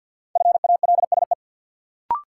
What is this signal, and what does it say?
Morse code, heard as a keyed tone, about 52 wpm, 700 Hz. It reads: FALSE
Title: Sentences from Top 1600 Words - New Word Drill Alt - 50wpm